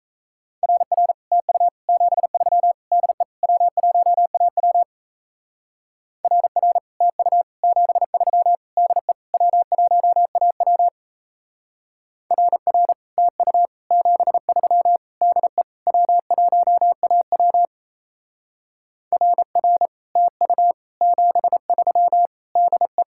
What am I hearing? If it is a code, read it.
Morse code, heard as a keyed tone, about 34 wpm, 700 Hz. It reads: RR TU 73 DE W1AW RR TU 73 DE W1AW RR TU 73 DE W1AW RR TU 73 DE
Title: Speed-Racing Common QSO Phrases - 28wpm